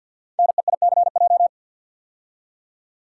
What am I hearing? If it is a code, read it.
Morse code, heard as a keyed tone, about 48 wpm, 690 Hz. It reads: NEICJ
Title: Code Groups - 5 Letters - 50wpm